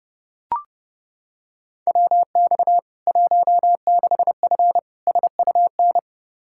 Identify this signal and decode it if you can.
Morse code, heard as a keyed tone, about 30 wpm, 700 Hz. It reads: WX 16F SUN
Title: K1USN SST QSO Practice 30wpm